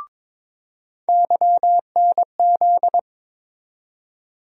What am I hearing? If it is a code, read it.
Morse code, heard as a keyed tone, about 22 wpm, 700 Hz. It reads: YNZ
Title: USA Call Sign Postfixes - 3 Letters 22wpm